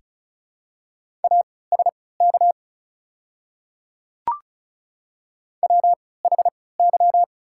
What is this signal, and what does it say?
Morse code, heard as a keyed tone, about 36 wpm, spaced as for 20 wpm, 700 Hz. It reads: ASK WHY
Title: 3x Character Spacing - Top 200 Words - 35wpm